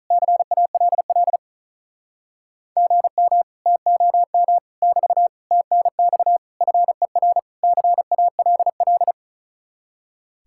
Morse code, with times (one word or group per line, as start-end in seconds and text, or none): 0.1-1.4 CALL
2.8-3.4 GM
3.7-4.6 TOM
4.8-5.3 <BT>
5.5-6.4 TNX
6.6-7.4 FER
7.6-9.1 CALL